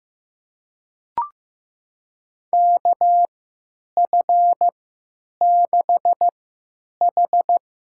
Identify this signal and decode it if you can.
Morse code, heard as a keyed tone, about 15 wpm, spaced as for 9 wpm, 700 Hz. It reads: KF6H
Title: USA Call Signs - 6 Characters - 3x Character Spacing - 15wpm